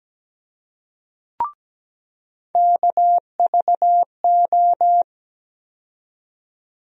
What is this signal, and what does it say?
Morse code, heard as a keyed tone, about 17 wpm, 700 Hz. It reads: KVO
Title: USA Call Sign Postfixes - 3 Letters 17wpm